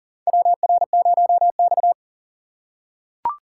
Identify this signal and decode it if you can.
Morse code, heard as a keyed tone, about 40 wpm, 700 Hz. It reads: WR0X
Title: USA Call Signs - 4 Characters - 4x Character Spacing - 40wpm